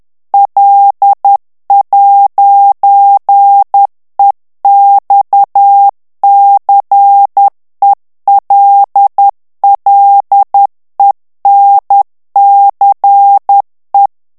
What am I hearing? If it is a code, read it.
Morse code, heard as a keyed tone, about 10 wpm, 800 Hz. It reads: L'EXCELLENCE
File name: FRANCAIS TEXTE CLAIR 2 VITESSE 10 MOTS MINUTES